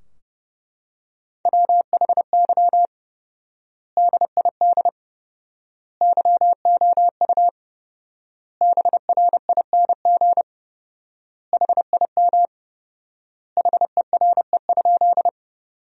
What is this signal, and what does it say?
Morse code, heard as a keyed tone, about 30 wpm, 700 Hz. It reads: WHY DID YOU BRING HIM HERE?
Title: 4x Word-Spacing - Sentences from Top 500 Words - 30wpm